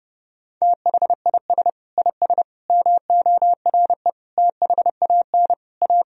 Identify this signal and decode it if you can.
Morse code, heard as a keyed tone, about 30 wpm, 700 Hz. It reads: THIS IS MORE THAN A
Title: Sentences from Top 1000 Words - Review Part 3 - 30wpm